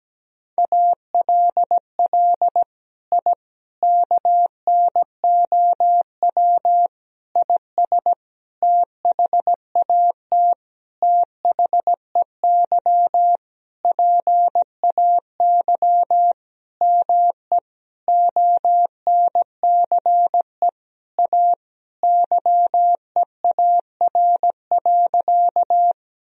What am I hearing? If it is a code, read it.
Morse code, heard as a keyed tone, about 17 wpm, 700 Hz. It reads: ALL I KNOW IS THAT THEY PAY ME ONCE A YEAR.
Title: Sentences from Top 1000 Words - Review Part 1 - 17wpm